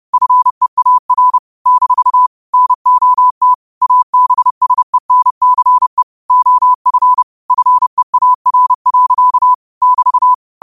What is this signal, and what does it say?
Morse code, heard as a keyed tone, about 30 wpm, 1000 Hz. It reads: FEAR - NOT ABSENCE OF FEAR. <BT>
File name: qotdfiles / 30wpm